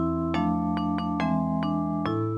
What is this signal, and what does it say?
There is no code.